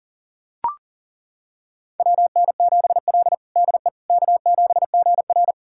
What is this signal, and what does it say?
Morse code, heard as a keyed tone, about 37 wpm, 700 Hz. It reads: WN7L DE K7GR
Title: USA Callsign DE Callsign - 4 Characters - 40wpm